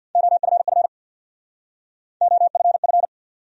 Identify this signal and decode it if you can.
Morse code, heard as a keyed tone, about 62 wpm, 700 Hz. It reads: OFF OFF
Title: Speed-Racing Top 200 Words 50wpm